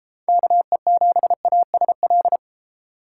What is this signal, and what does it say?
Morse code, heard as a keyed tone, about 30 wpm, 700 Hz. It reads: KE7ASL